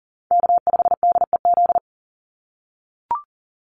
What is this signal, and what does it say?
Morse code, heard as a keyed tone, about 40 wpm, 700 Hz. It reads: K5DEZ
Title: NA SOTA Chaser Callsigns 40wpm